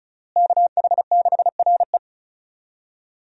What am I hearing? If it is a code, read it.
Morse code, heard as a keyed tone, about 36 wpm, 700 Hz. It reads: KH6RE
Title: USA Call Signs - 5 Characters - Repeated 2x - 35wpm